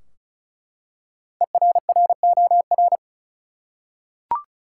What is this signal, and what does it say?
Morse code, heard as a keyed tone, about 34 wpm, 700 Hz. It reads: ERROR
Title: Five Letter Words - 35wpm